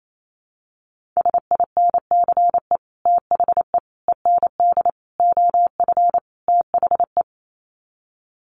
Morse code, heard as a keyed tone, about 28 wpm, 700 Hz. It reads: SINCE THE END OF THE
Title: Sets of 5 Words Encore 28wpm